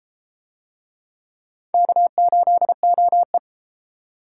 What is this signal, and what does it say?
Morse code, heard as a keyed tone, about 29 wpm, 700 Hz. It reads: K8OE